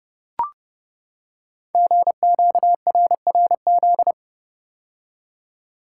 Morse code, heard as a keyed tone, about 28 wpm, 700 Hz. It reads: GQRRZ
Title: Code Groups - 5 Letters - 30wpm